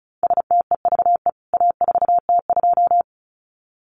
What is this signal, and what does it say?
Morse code, heard as a keyed tone, about 34 wpm, 700 Hz. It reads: STEVE A4T2
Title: CWT Exchange 35wpm